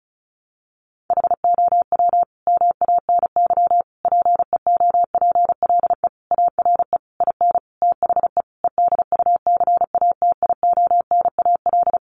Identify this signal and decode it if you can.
Morse code, heard as a keyed tone, about 34 wpm, 700 Hz. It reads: HOW MANY PEOPLE ARE IN THE EDUCATIONAL
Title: Sentences from Top 1200 Words 35wpm